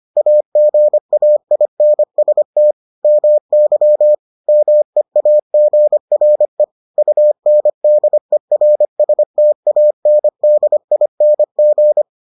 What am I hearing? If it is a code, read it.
Morse code, heard as a keyed tone, about 25 wpm, 600 Hz. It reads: AGAINST MY MEAGRE UNDERSTANDING